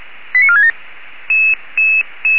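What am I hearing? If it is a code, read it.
Five-tone selective call (ZVEI), heard as quick runs of five tones, tones of about 70 ms. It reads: 89467
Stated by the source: ZVEI five-tone selective call received on an SDR.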